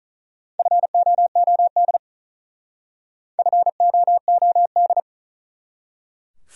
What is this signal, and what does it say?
Morse code, heard as a keyed tone, about 45 wpm, 700 Hz. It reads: FOOD FOOD